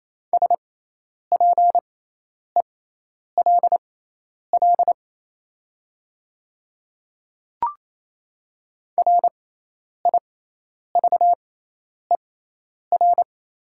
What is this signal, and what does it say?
Morse code, heard as a keyed tone, about 28 wpm, spaced as for 10 wpm, 700 Hz. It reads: SPELL RIVER
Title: 6x Character Spacing - Top 300 Words - 28wpm